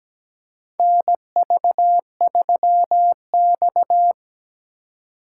Morse code, heard as a keyed tone, about 17 wpm, 700 Hz. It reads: NV3X
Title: USA Call Signs - 4 Characters - 3x Character Spacing - 17wpm